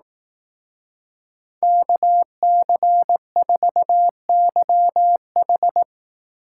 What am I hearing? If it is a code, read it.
Morse code, heard as a keyed tone, about 18 wpm, 700 Hz. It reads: KC4YH
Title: Speed Racing - USA 2x2 Call Signs 15wpm